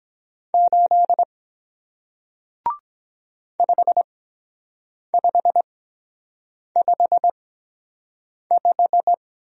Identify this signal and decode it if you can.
Morse code, heard as a keyed tone, about 22 wpm, spaced as for 7 wpm, 700 Hz. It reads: 8 5555